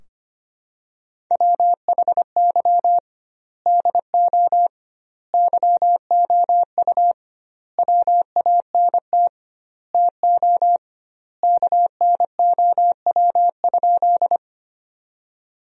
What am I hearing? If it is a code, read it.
Morse code, heard as a keyed tone, about 25 wpm, 700 Hz. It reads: WHY DO YOU WANT TO KNOW?
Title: 2x Word-Spacing - Sentences from Top 200 Words - 25wpm